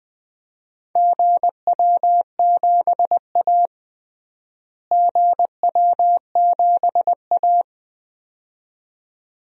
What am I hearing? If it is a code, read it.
Morse code, heard as a keyed tone, about 20 wpm, 700 Hz. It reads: GW7A GW7A
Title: DXCC-100 Call Signs - 4 Characters - Repeated 2x - 20wpm